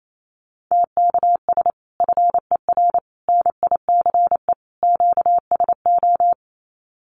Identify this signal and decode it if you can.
Morse code, heard as a keyed tone, about 28 wpm, 700 Hz. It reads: TKS FER NICE QSO
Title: Common QSO Phrases 28wpm